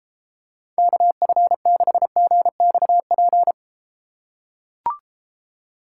Morse code, heard as a keyed tone, about 32 wpm, 700 Hz. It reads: KF6GXP